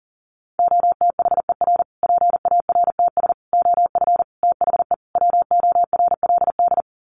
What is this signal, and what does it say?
Morse code, heard as a keyed tone, about 40 wpm, 700 Hz. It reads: OTHER PARTS OF THE WORLD
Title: Sets of 5 Words Encore 40wpm